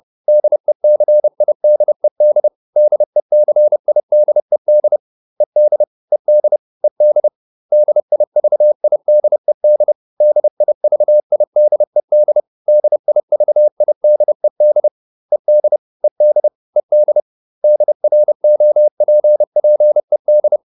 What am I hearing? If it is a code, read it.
Morse code, heard as a keyed tone, about 30 wpm, 600 Hz. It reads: DECIDED DECIDED ED ED ED DIVIDED DIVIDED DIVIDED ED ED ED DROPPED